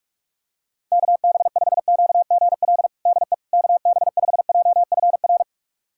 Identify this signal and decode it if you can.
Morse code, heard as a keyed tone, about 46 wpm, 700 Hz. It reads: KB5QGL DE KB5JFR
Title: USA Callsign DE Callsign - 6 Characters - 45wpm